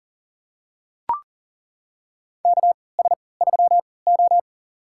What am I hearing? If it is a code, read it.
Morse code, heard as a keyed tone, about 37 wpm, spaced as for 22 wpm, 700 Hz. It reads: KS3O
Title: USA Call Signs - 4 Characters - 3x Character Spacing - 40wpm